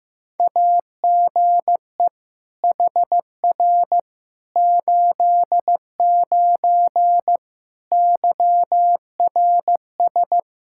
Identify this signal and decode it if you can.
Morse code, heard as a keyed tone, about 16 wpm, 700 Hz. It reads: AGE HR 89 YRS
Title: Practice QSO - Exchange 3 - More Info - 15wpm